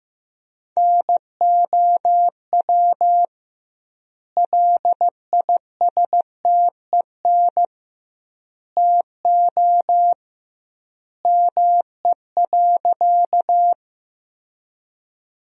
Morse code, heard as a keyed tone, about 15 wpm, 700 Hz. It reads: NOW LISTEN TO ME.
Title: 2x Word-Spacing - Sentences from Top 400 Words - 15wpm